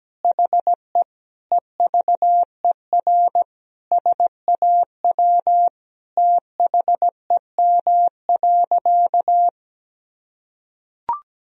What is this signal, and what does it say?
Morse code, heard as a keyed tone, about 17 wpm, 700 Hz. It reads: HE EVER SAW THEM.
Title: Sentences from Top 1000 Words - Review Part 2 - 17wpm